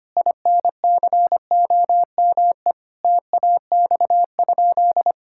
Morse code, heard as a keyed tone, about 25 wpm, 700 Hz. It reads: INCOME TAX?
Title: Sentences from Top 1000 Words - Review Part 5 - 25wpm